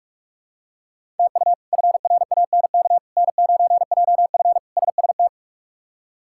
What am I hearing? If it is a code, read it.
Morse code, heard as a keyed tone, about 46 wpm, 700 Hz. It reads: TU FRANK N9JF SST
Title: K1USN SST QSO Practice - 45wpm